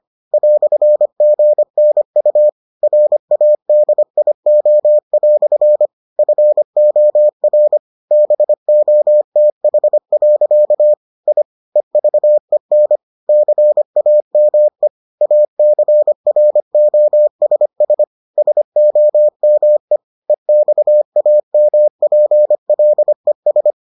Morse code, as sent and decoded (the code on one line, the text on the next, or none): .-..-. --. -. ..- / .-. .- -.. .. --- .-..-. / ..-. --- .-. / -... --- - .... .-.-.- / .. / . ...- . -. / -.-. .- -- . / .- -.-. .-. --- ... ... / ... --- -- . / . -..- .- -- .--. .-.. ...
"GNU RADIO" FOR BOTH. I EVEN CAME ACROSS SOME EXAMPLES